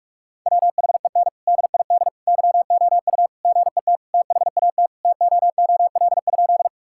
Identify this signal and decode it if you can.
Morse code, heard as a keyed tone, about 46 wpm, 700 Hz. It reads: WHEN DID YOU GET THAT TOOL?
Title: Sentences from Top 1000 Words - Review Part 6 - 45wpm